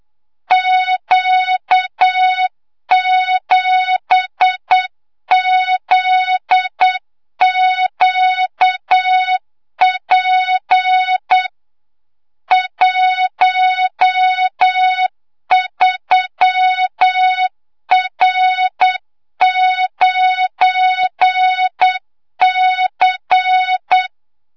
Morse code, heard as a keyed tone, about 8 wpm, 3700 Hz, with noise. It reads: Q7ZQP 13R9C
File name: Morze-HA2MN / hangp / 2 06 GYAKORLAT